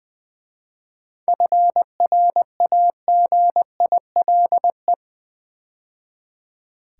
Morse code, heard as a keyed tone, about 20 wpm, 700 Hz. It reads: FRAGILE
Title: 7 Letter Words - First 3 Letters Given - 20wpm